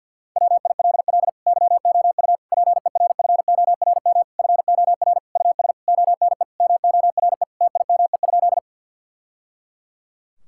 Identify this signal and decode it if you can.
Morse code, heard as a keyed tone, about 50 wpm, 700 Hz. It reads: WILL YOU PERFORM FOR US ONE MORE TIME?
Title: Sentences from Top 1000 Words - Review Part 5 - 50wpm